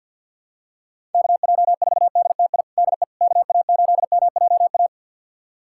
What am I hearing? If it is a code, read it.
Morse code, heard as a keyed tone, about 47 wpm, 700 Hz. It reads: KJ4DTI DE KA8MJA
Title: USA Callsign DE Callsign 50wpm